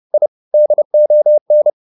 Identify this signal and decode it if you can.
Morse code, heard as a keyed tone, about 30 wpm, 600 Hz. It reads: I DON
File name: the old man and the sea / the old man and the sea 1 30wpm